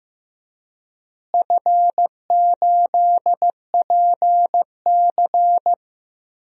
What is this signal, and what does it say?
Morse code, heard as a keyed tone, about 15 wpm, 700 Hz. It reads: F8PC
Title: DXCC-100 Call Signs - 4 Characters - Repeated 3x - 15wpm